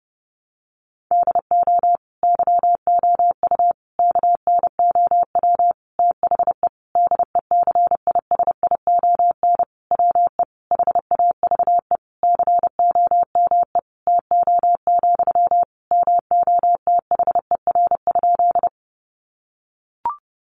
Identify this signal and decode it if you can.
Morse code, heard as a keyed tone, about 30 wpm, 700 Hz. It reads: DO YOU KNOW THE DECISION WE HAVE COME TO, MOTHER?
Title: Sentences from Top 1000 Words 30wpm